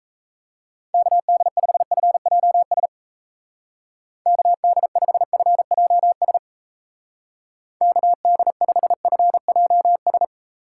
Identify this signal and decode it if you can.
Morse code, heard as a keyed tone, about 35 wpm, 700 Hz. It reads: KD5FJS KD5FJS KD5FJS
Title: Speed Racing - USA 2x3 Call Signs 28wpm